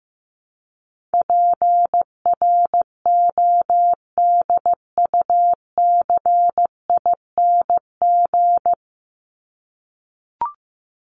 Morse code, heard as a keyed tone, about 15 wpm, 700 Hz. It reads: PRODUCING
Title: ING Suffix - Top 100 Words with Base Words 15wpm